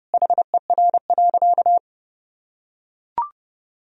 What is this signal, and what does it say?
Morse code, heard as a keyed tone, about 30 wpm, 700 Hz. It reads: HER.
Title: Sentences from Top 1000 Words - Review Part 3 - 30wpm